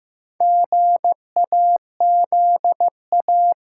Morse code, heard as a keyed tone, about 15 wpm, 700 Hz. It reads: GAZA
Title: Sets of 5 Words - New Word Drill - 15wpm